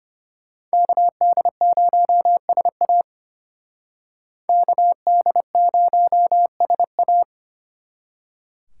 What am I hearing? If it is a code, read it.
Morse code, heard as a keyed tone, about 30 wpm, 700 Hz. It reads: KD0SA KD0SA
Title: Speed Racing - USA 2x2 Call Signs 25wpm